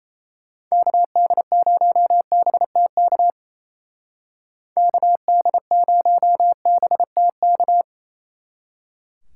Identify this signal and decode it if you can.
Morse code, heard as a keyed tone, about 35 wpm, 700 Hz. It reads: KD0BTK KD0BTK
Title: Speed Racing - USA 2x3 Call Signs 28wpm